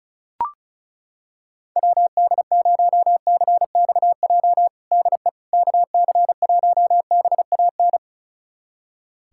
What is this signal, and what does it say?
Morse code, heard as a keyed tone, about 36 wpm, 700 Hz. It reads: WD0CXJ DE KC1BAN
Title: USA Callsign DE Callsign - 6 Characters - 35wpm